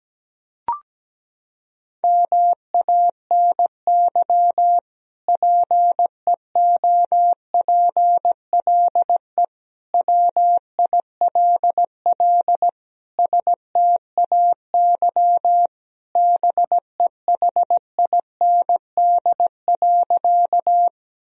Morse code, heard as a keyed tone, about 17 wpm, 700 Hz. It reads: MANY PEOPLE WILL STAY BEHIND.
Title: Sentences from Top 1000 Words - Review Part 1 - 17wpm